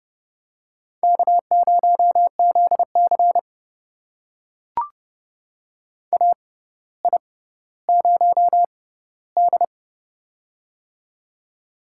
Morse code, heard as a keyed tone, about 30 wpm, spaced as for 15 wpm, 700 Hz. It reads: K0ZC AI0D
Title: USA Call Signs - 4 Characters - 6x Character Spacing - 30wpm